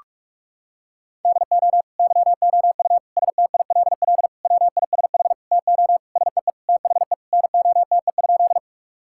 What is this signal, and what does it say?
Morse code, heard as a keyed tone, about 41 wpm, 700 Hz. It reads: DO YOU STILL WISH TO SEE THE NOTE?
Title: Sentences from Top 1600 Words 45wpm